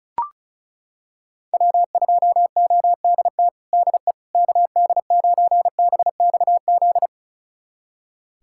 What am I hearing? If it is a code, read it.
Morse code, heard as a keyed tone, about 34 wpm, 700 Hz. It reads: W2ODT DE KD9BXZ
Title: USA Callsign DE Callsign 35wpm